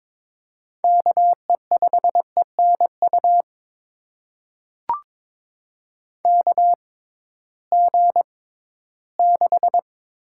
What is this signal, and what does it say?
Morse code, heard as a keyed tone, about 23 wpm, spaced as for 14 wpm, 700 Hz. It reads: KE5ENU KG6
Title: USA Call Signs - 6 Characters - 6x Character Spacing - 22wpm